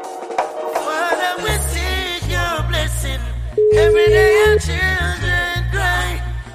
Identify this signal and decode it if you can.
Ringback tone (Europe, 425 Hz), with noise.